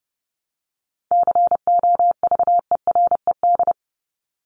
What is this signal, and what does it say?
Morse code, heard as a keyed tone, about 30 wpm, 700 Hz. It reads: COVERED
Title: ED Suffix - Top 100 Words with Base Words 30wpm